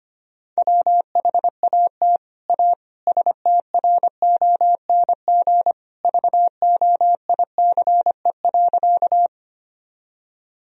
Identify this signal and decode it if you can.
Morse code, heard as a keyed tone, about 25 wpm, 700 Hz. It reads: WHAT A STRONG VOICE.
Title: Sentences from Top 1000 Words - Review Part 3 - 25wpm